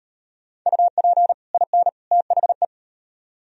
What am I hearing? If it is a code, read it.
Morse code, heard as a keyed tone, about 38 wpm, 700 Hz. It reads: UP IN THE